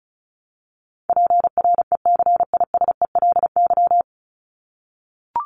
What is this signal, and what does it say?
Morse code, heard as a keyed tone, about 34 wpm, 700 Hz. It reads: PRECISELY
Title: LY Suffix - Top 100 Words with Base Words 35wpm